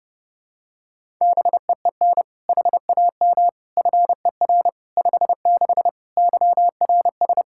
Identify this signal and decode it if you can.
Morse code, heard as a keyed tone, about 30 wpm, 700 Hz. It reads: BEEN HAM FER 56 YRS
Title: Practice QSO - Exchange 3 - More Info - 30wpm